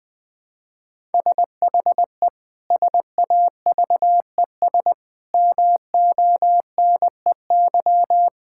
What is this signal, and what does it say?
Morse code, heard as a keyed tone, about 20 wpm, 700 Hz. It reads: SHE SAVES MONEY
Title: Difficult Sentences with Hints - 20wpm